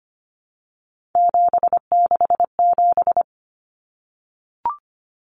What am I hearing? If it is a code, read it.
Morse code, heard as a keyed tone, about 25 wpm, 700 Hz. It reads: 767
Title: Numbers - 3-Digit 25wpm